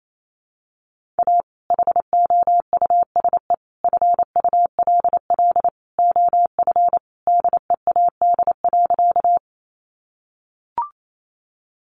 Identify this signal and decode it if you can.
Morse code, heard as a keyed tone, about 28 wpm, 700 Hz. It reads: A HOUSE FULL OF DEAD.
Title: Sentences from Top 1500 Words 28wpm